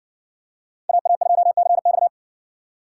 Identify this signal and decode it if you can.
Morse code, heard as a keyed tone, about 65 wpm, 700 Hz. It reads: AA2CX